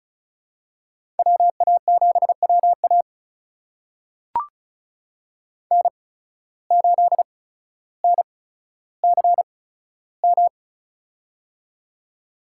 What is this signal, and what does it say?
Morse code, heard as a keyed tone, about 36 wpm, spaced as for 15 wpm, 700 Hz. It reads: WA7WA N8NCM
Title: USA Call Signs - 5 Characters - 8x Character Spacing - 35wpm